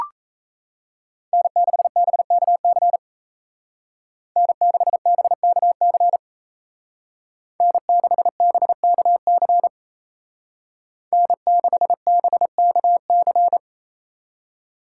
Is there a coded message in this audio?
Yes: Morse code from 1.3 to 13.6 s, heard as a keyed tone, about 36 wpm, 700 Hz. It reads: N6BKC N6BKC N6BKC N6BKC